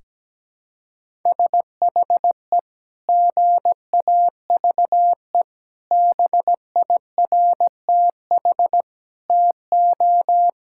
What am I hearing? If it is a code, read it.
Morse code, heard as a keyed tone, about 17 wpm, 700 Hz. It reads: SHE GAVE BIRTH TO
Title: Sentences from Top 1000 Words - Review Part 5 - 17wpm